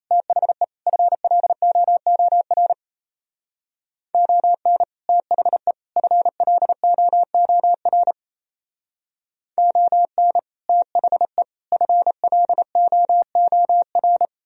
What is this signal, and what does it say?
Morse code, heard as a keyed tone, about 34 wpm, 700 Hz. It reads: THE FLOOR ON THE FLOOR ON THE FLOOR
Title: Speed-Racing - Sets of 3 Words 28wpm